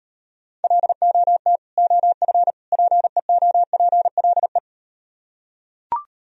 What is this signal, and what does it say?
Morse code, heard as a keyed tone, about 38 wpm, 700 Hz. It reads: LOT OF PEOPLE